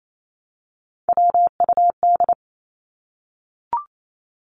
Morse code, heard as a keyed tone, about 28 wpm, 700 Hz. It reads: WUD
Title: Abbreviations 28wpm